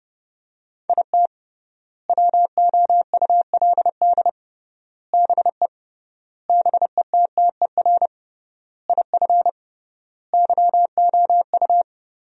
Morse code, heard as a keyed tone, about 30 wpm, 700 Hz. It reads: IT WOULD BE BETTER IF YOU
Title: 3x Word-Spacing - Sentences from Top 400 Words - 30wpm